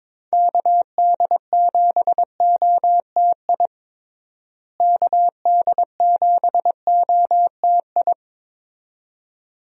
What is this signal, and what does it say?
Morse code, heard as a keyed tone, about 22 wpm, 700 Hz. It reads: KD7OTI KD7OTI
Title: USA Call Signs - 6 Characters - Repeated 2x - 22wpm